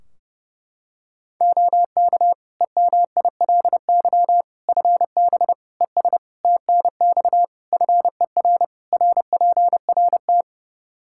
Morse code, heard as a keyed tone, about 30 wpm, 700 Hz. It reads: OK EMILY FB ES TNX FER RPRT
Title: Practice QSO - Exchange 2 - Info - 30wpm